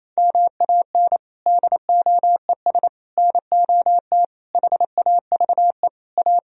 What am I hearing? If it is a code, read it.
Morse code, heard as a keyed tone, about 28 wpm, 700 Hz. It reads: MAN DOES NOT HAVE A
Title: Sentences from Top 1000 Words - Review Part 3 - 28wpm